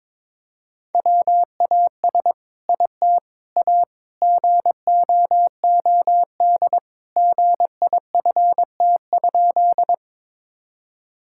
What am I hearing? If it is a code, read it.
Morse code, heard as a keyed tone, about 22 wpm, 700 Hz. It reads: WAS IT A GOOD GIFT?
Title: Sentences from Top 1000 Words - Review Part 6 - 22wpm